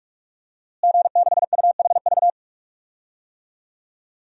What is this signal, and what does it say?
Morse code, heard as a keyed tone, about 43 wpm, 700 Hz. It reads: G6UHV
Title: DXCC-100 Call Signs - 5 Characters - Repeated 3x - 45wpm